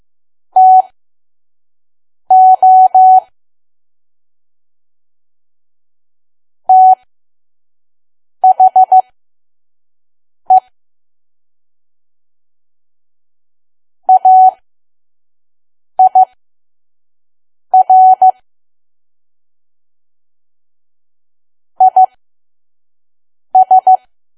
Morse code, heard as a keyed tone, about 15 wpm, spaced as for 5 wpm, 750 Hz. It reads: TO THE AIR IS
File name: Archive / 5 WPM files / PM